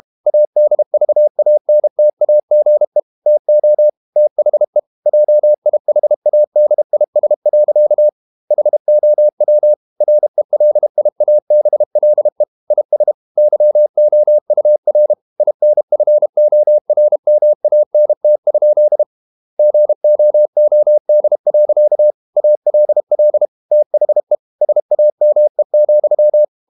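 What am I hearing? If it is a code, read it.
Morse code, heard as a keyed tone, about 32 wpm, 600 Hz. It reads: ADVANTAGE TO THE JIHADIS. HOW RELIABLE IS YOUR INFORMANT? GOOD. ALL THE SAME,
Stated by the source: Morse code practice at 32 wpm from CW med Gnister, reading Ken Follett's novel Never in English.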